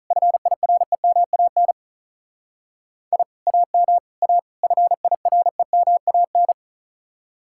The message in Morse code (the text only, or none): FIREMAN I AM A FIREMAN